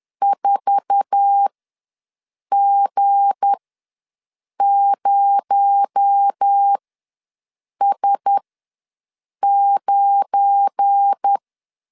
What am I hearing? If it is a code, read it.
Morse code, heard as a keyed tone, about 10 wpm, spaced as for 6 wpm, 800 Hz. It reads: 4G0S9
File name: LEZIONI / 8 YJ45 / 8 YJ45 ws 10wpm 3843